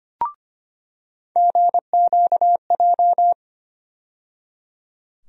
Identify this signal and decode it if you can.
Morse code, heard as a keyed tone, about 25 wpm, 700 Hz. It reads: GQJ
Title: USA Call Sign Postfixes - 3 Letters 25wpm